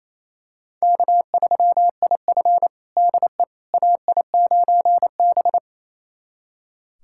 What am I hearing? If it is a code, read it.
Morse code, heard as a keyed tone, about 27 wpm, 700 Hz. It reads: K3IF DE AI9B
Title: USA Callsign DE Callsign - 4 Characters - 28wpm